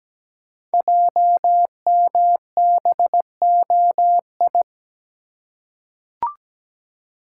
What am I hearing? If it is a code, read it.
Morse code, heard as a keyed tone, about 17 wpm, 700 Hz. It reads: JMBOI
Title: Code Groups - 5 Letters - 17wpm